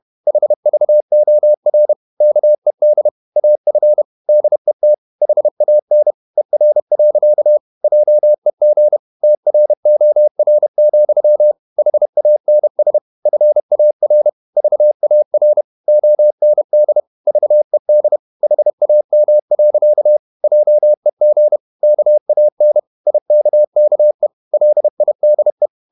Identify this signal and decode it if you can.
Morse code, heard as a keyed tone, about 31 wpm, 600 Hz. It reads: HVOR KED AF DET HAN ER. JEG TROR, HANS FAR VAR OND VED HAM. JEG KAN IKKE LIDE
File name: Ferie QRQ 31wpm